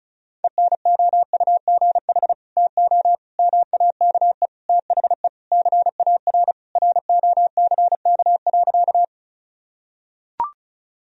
Morse code, heard as a keyed tone, about 36 wpm, 700 Hz. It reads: ENOUGH TO MAKE THE CAR ROCK.
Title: Sentences from Top 1000 Words - Review Part 1 - 35wpm